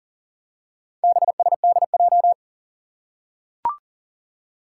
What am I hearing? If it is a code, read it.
Morse code, heard as a keyed tone, about 38 wpm, 700 Hz. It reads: BSDJ
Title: Four Challenging Letter-Number Combinations 40wpm